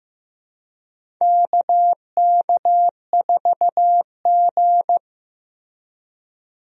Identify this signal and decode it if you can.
Morse code, heard as a keyed tone, about 15 wpm, 700 Hz. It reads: KK4G
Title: USA Call Signs - 4 Characters - Repeated 3x - 15wpm